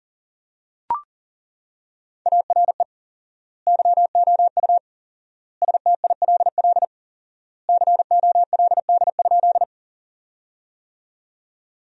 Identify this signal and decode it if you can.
Morse code, heard as a keyed tone, about 40 wpm, 700 Hz. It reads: ARE YOU STILL COLD?
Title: 4x Word-Spacing - Sentences from Top 400 Words - 40wpm